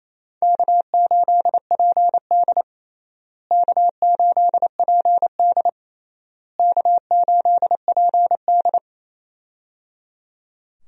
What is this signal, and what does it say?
Morse code, heard as a keyed tone, about 28 wpm, 700 Hz. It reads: K8PD K8PD K8PD
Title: USA Call Signs - 4 Characters - Repeated 3x - 28wpm